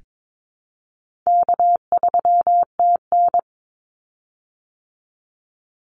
Morse code, heard as a keyed tone, about 22 wpm, 700 Hz. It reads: K3TN
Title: Easy Call Signs Repeated 2x - 22wpm